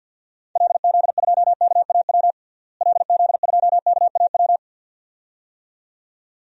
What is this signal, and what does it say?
Morse code, heard as a keyed tone, about 50 wpm, 700 Hz. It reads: LZ2KAW LZ2KAW
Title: DXCC-100 Call Signs - 6 Characters - Repeated 2x - 50wpm